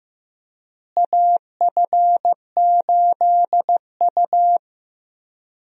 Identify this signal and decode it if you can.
Morse code, heard as a keyed tone, about 15 wpm, 700 Hz. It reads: AF8U
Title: USA Call Signs - 4 Characters - Repeated 2x - 15wpm